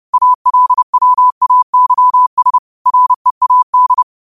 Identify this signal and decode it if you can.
Morse code, heard as a keyed tone, about 30 wpm, 1000 Hz. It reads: ALWAYS READ
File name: qotdfiles / 30wpm